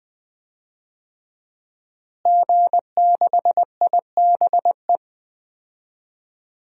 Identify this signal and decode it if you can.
Morse code, heard as a keyed tone, about 20 wpm, 700 Hz. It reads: G6IBE
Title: DXCC-100 Call Signs - 5 Characters - Repeated 2x - 20wpm